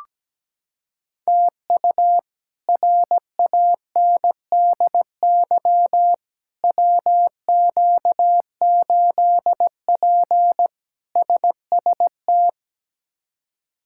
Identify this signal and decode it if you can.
Morse code, heard as a keyed tone, about 17 wpm, 700 Hz. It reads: TU RANDY WQ8P SST
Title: K1USN SST QSO Practice - 17wpm